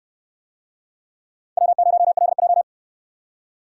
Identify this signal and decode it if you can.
Morse code, heard as a keyed tone, about 69 wpm, 700 Hz. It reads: W0LJ